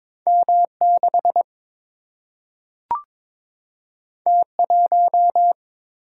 Morse code, heard as a keyed tone, about 22 wpm, 700 Hz. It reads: M6 T1
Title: Single-Letter Single-Number 22wpm